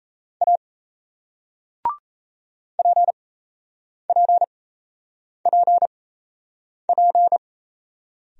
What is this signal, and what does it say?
Morse code, heard as a keyed tone, about 36 wpm, spaced as for 8 wpm, 700 Hz. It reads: A PPPP